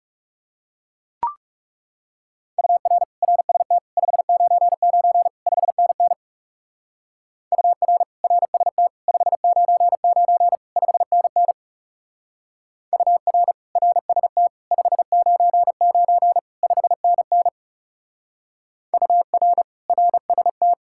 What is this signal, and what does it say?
Morse code, heard as a keyed tone, about 37 wpm, 700 Hz. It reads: UR RST 599 5NN UR RST 599 5NN UR RST 599 5NN UR RST
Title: Speed-Racing Common QSO Phrases - 30wpm